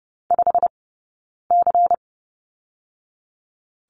Morse code, heard as a keyed tone, about 30 wpm, 700 Hz. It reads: 5 C
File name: 092 Lesson 16d - Letter c - Two Characters 30wpm